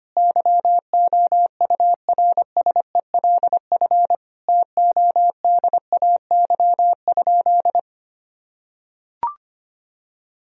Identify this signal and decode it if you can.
Morse code, heard as a keyed tone, about 25 wpm, 700 Hz. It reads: YOURSELF TODAY?
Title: Sentences from Top 1000 Words - Review Part 4 - 25wpm